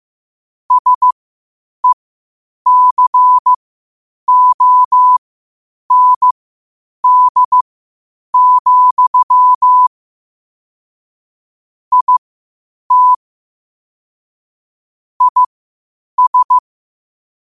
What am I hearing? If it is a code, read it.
Morse code, heard as a keyed tone, about 15 wpm, spaced as for 8 wpm, 1000 Hz. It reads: SECOND, IT IS
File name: qotdfiles / 8wpm